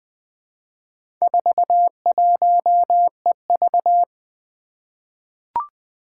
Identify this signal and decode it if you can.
Morse code, heard as a keyed tone, about 21 wpm, 700 Hz. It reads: 41EV
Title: Four Challenging Letter-Number Combinations 20wpm